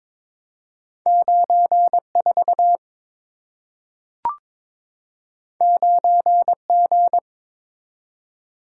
Morse code, heard as a keyed tone, about 22 wpm, 700 Hz. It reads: 94 9G
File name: practice / 22wpm / Two Letter-Number Combinations - 22wpm